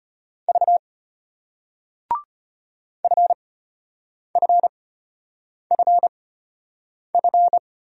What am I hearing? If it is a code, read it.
Morse code, heard as a keyed tone, about 34 wpm, spaced as for 8 wpm, 700 Hz. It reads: V FFFF